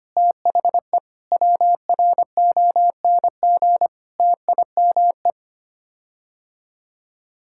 Morse code, heard as a keyed tone, about 25 wpm, 700 Hz. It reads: THE WRONG TIME
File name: Sets of 9 Words - 25wpm